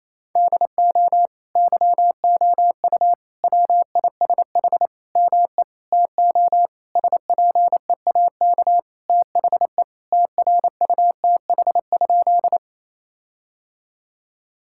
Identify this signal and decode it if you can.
Morse code, heard as a keyed tone, about 27 wpm, 700 Hz. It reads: DO YOU WISH ME TO SPEAK THE TRUTH?
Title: Sentences from Top 1600 Words 28wpm